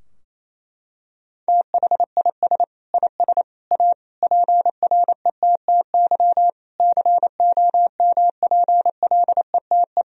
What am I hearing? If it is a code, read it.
Morse code, heard as a keyed tone, about 28 wpm, 700 Hz. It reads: THIS IS A PRETTY COMPLETE
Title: Sentences from Top 1000 Words - Review Part 2 - 28wpm